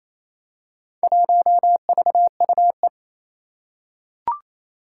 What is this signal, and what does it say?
Morse code, heard as a keyed tone, about 28 wpm, 700 Hz. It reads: 1VUE
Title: Four Challenging Letter-Number Combinations 28wpm